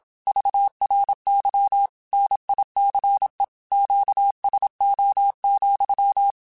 Morse code, heard as a keyed tone, about 27 wpm, 800 Hz. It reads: VRY NICE QSO,
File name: QSOs / QSOs-mp3s / WA7QOX 2@25